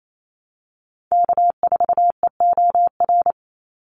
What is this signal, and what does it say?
Morse code, heard as a keyed tone, about 28 wpm, 700 Hz. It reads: K4EOR
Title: Call Signs - 28wpm